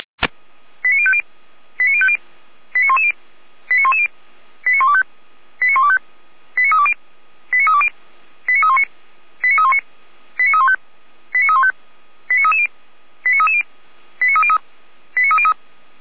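Five-tone selective call (ZVEI), heard as quick runs of five tones, tones of about 70 ms. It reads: 89050 89050 89110 89110 89125 89125 89320 89320 89319 89319 89315 89315 89330 89330 89303 89303
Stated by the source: ZVEI five-tone call, code 89229, received by SDR.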